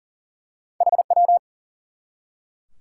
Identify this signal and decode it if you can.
Morse code, heard as a keyed tone, about 40 wpm, 700 Hz. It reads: HW
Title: Top 180 QSO Elements 40wpm